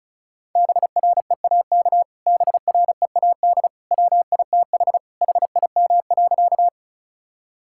Morse code, heard as a keyed tone, about 36 wpm, 700 Hz. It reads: BREAK BREAD WITH HIM.
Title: Sentences from Top 1600 Words 35wpm